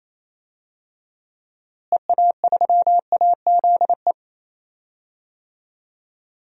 Morse code, heard as a keyed tone, about 28 wpm, 700 Hz. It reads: EA3AZE
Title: DXCC-100 Call Signs - 6 Characters - Repeated 3x - 28wpm